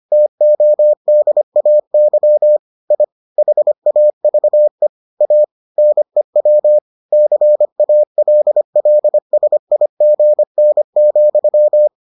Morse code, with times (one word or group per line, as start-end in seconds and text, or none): 0.1-2.6 TODAY
2.9-3.0 I
3.4-4.9 HAVE
5.2-5.4 A
5.8-6.8 NEW
7.1-11.9 CALLSIGN,